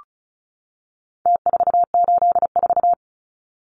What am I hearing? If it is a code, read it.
Morse code, heard as a keyed tone, about 34 wpm, 700 Hz. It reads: T484
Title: Numbers - Contest Style - 4-Digit 35wpm